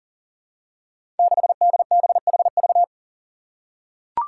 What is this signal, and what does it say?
Morse code, heard as a keyed tone, about 43 wpm, 700 Hz. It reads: 6DBHV
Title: Five Challenging Letter-Number Combinations 40wpm